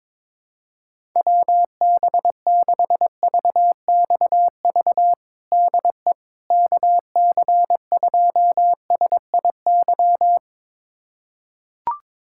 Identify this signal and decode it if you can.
Morse code, heard as a keyed tone, about 22 wpm, 700 Hz. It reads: WB6VXV DE KC2SIY
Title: USA Callsign DE Callsign 22wpm